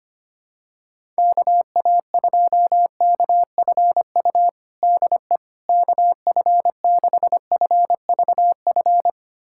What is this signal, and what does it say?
Morse code, heard as a keyed tone, about 25 wpm, 700 Hz. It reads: KA2KFU DE KF6FVF
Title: USA Callsign DE Callsign - 6 Characters - 25wpm